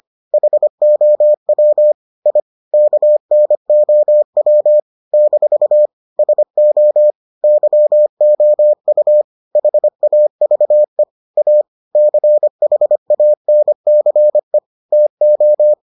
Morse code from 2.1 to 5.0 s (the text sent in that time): I KNOW